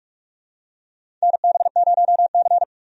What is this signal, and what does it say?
Morse code, heard as a keyed tone, about 46 wpm, 700 Hz. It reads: NB0C
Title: USA Call Signs - 4 Characters - Repeated 3x - 45wpm